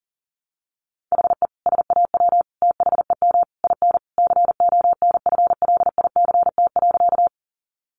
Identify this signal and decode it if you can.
Morse code, heard as a keyed tone, about 40 wpm, 700 Hz. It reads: HE SAW THEM IN CONFLICT.
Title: Sentences from Top 1300 Words 40wpm